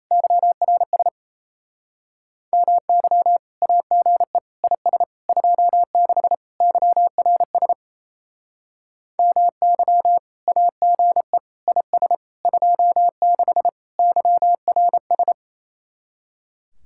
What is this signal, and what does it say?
Morse code, heard as a keyed tone, about 38 wpm, 700 Hz. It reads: YRS MY AGE IS 26 YRS MY AGE IS 26 YRS